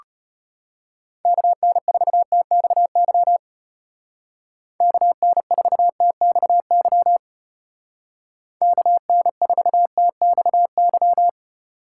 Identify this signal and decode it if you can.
Morse code, heard as a keyed tone, about 33 wpm, 700 Hz. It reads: KN4TXY KN4TXY KN4TXY